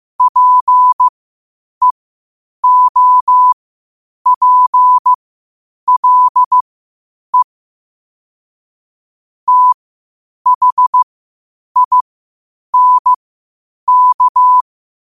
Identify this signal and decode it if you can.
Morse code, heard as a keyed tone, about 15 wpm, spaced as for 8 wpm, 1000 Hz. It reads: PEOPLE THINK
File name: qotdfiles / 8wpm